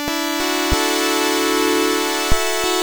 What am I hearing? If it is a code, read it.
There is no code.